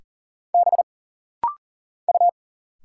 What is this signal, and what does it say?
Morse code, heard as a keyed tone, about 40 wpm, 700 Hz. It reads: B U
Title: Single Letters - Rapid-Fire 40wpm